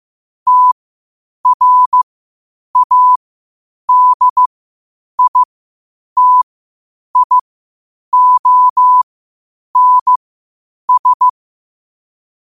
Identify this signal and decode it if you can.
Morse code, heard as a keyed tone, about 15 wpm, spaced as for 8 wpm, 1000 Hz. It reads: TRADITIONS
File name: qotdfiles / 8wpm